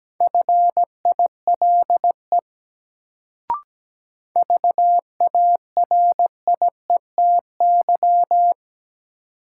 Morse code, heard as a keyed tone, about 17 wpm, 700 Hz. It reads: FILE VARIETY
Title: Sentences from Top 1000 Words - Review Part 4 - New Word Drill - 17wpm